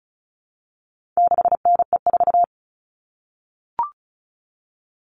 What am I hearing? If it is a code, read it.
Morse code, heard as a keyed tone, about 34 wpm, 700 Hz. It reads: 6NE4
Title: All Cut Numbers - 4-Digit 35wpm